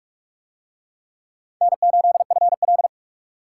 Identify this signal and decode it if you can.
Morse code, heard as a keyed tone, about 40 wpm, 700 Hz. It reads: N8FL